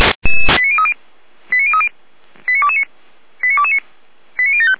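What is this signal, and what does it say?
Five-tone selective call (ZVEI), heard as quick runs of five tones, tones of about 70 ms. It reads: 89030 89030 89229 89229 89075
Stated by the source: Five-tone ZVEI selective call received on SDR.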